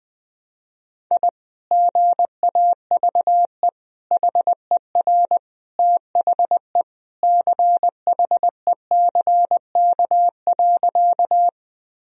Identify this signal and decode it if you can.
Morse code, heard as a keyed tone, about 20 wpm, 700 Hz. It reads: I GAVE HER THE CHECK.
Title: Sentences from Top 1000 Words - Review Part 3 - 20wpm